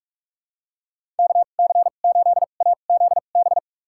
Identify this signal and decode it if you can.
Morse code, heard as a keyed tone, about 46 wpm, spaced as for 32 wpm, 700 Hz. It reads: KC8AZB
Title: USA Call Signs - 6 Characters - 2x Character Spacing - 45wpm